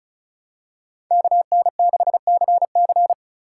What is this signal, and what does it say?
Morse code, heard as a keyed tone, about 34 wpm, 700 Hz. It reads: KN6CC
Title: USA Call Signs - 5 Characters - 3x Character Spacing - 35wpm